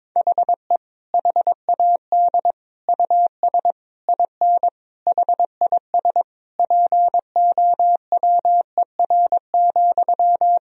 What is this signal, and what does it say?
Morse code, heard as a keyed tone, about 22 wpm, 700 Hz. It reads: HE HAD US IN HIS POWER,
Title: Sentences from Top 1000 Words - Review Part 2 - 22wpm